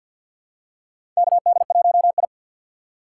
Morse code, heard as a keyed tone, about 53 wpm, 700 Hz. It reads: KD1I
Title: USA Call Signs - 4 Characters - 6x Character Spacing - 50wpm